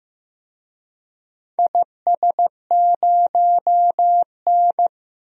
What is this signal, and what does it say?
Morse code, heard as a keyed tone, about 15 wpm, 700 Hz. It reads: IS0N
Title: DXCC-100 Call Signs - 4 Characters - Repeated 2x - 15wpm